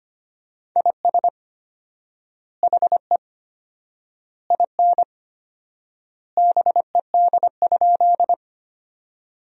Morse code, heard as a keyed tone, about 25 wpm, 700 Hz. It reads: IS HE IN BED?
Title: 4x Word-Spacing - Sentences from Top 500 Words - 25wpm